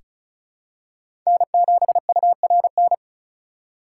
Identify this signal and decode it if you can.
Morse code, heard as a keyed tone, about 36 wpm, 700 Hz. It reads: N7URN
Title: USA Call Signs - 5 Characters - 3x Character Spacing - 35wpm